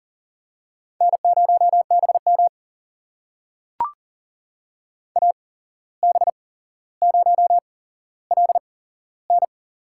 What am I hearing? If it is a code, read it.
Morse code, heard as a keyed tone, about 40 wpm, spaced as for 16 wpm, 700 Hz. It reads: N0BM AB0LN
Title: USA Call Signs - 8x Character Spacing - 40wpm